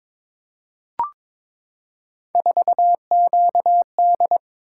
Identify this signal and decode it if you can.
Morse code, heard as a keyed tone, about 22 wpm, 700 Hz. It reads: 4QD
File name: practice / 22wpm / Three Letter-Number Combinations - 22wpm